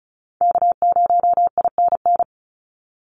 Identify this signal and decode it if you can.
Morse code, heard as a keyed tone, about 36 wpm, 700 Hz. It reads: K0INN
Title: Call Signs - 35wpm